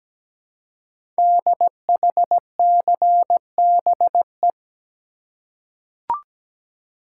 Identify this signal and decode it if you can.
Morse code, heard as a keyed tone, about 17 wpm, 700 Hz. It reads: DHCBE
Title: Code Groups - 5 Letters - 17wpm